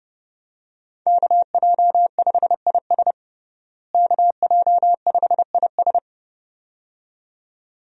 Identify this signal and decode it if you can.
Morse code, heard as a keyed tone, about 30 wpm, 700 Hz. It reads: KJ5IS KJ5IS
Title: USA Call Signs - 5 Characters - Repeated 2x - 30wpm